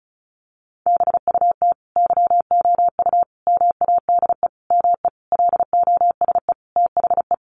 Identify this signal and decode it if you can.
Morse code, heard as a keyed tone, about 34 wpm, 700 Hz. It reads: BUT YOU MADE ME LOSE THE
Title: Sentences from Top 1300 Words 35wpm